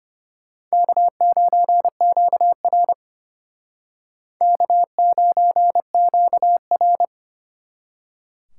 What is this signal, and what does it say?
Morse code, heard as a keyed tone, about 32 wpm, 700 Hz. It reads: K9QR K9QR